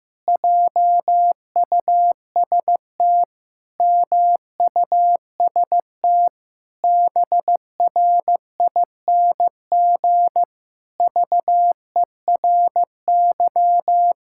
Morse code, heard as a keyed tone, about 15 wpm, 700 Hz. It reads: JUST MUST BRING VERY
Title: Pangram Challenge v2 - 15wpm